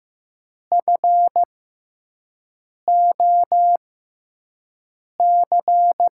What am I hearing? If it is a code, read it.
Morse code, heard as a keyed tone, about 15 wpm, spaced as for 5 wpm, 700 Hz. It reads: FOC
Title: USA Call Signs - 5 Characters - 6x Character Spacing - 15wpm